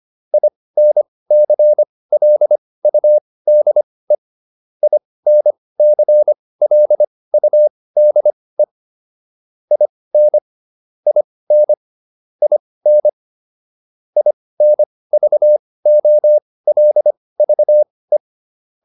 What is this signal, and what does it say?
Morse code, heard as a keyed tone, about 25 wpm, spaced as for 18 wpm, 600 Hz. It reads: INCLUDE INCLUDE IN IN IN INVOLVE